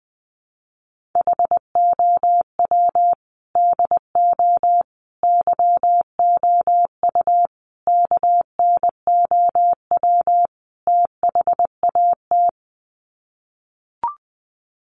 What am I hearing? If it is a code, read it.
Morse code, heard as a keyed tone, about 20 wpm, 700 Hz. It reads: HOW DO YOU KNOW THAT
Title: Sets of 5 Words Encore 20wpm